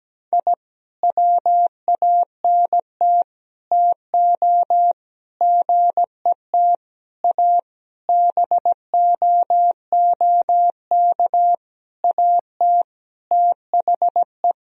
Morse code, heard as a keyed tone, about 17 wpm, 700 Hz. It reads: I WANT TO GET A BOOK AT THE
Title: Sentences from Top 1000 Words - Review Part 1 - 17wpm